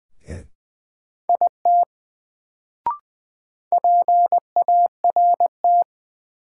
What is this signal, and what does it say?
Morse code, heard as a keyed tone, about 20 wpm, 700 Hz. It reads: IT PART
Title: Sets of 2 Words - New Word Drill - 20wpm